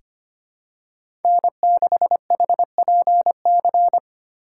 Morse code, heard as a keyed tone, about 25 wpm, 700 Hz. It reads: N6HPC
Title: USA Call Signs - 5 Characters - 6x Character Spacing - 25wpm